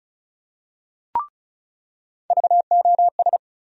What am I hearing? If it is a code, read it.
Morse code, heard as a keyed tone, about 34 wpm, 700 Hz. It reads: VOS
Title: USA Call Sign Postfixes - 3 Letters 35wpm